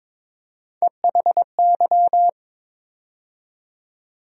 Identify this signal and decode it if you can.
Morse code, heard as a keyed tone, about 22 wpm, 700 Hz. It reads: EHY
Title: USA Call Sign Postfixes - 3 Letters 22wpm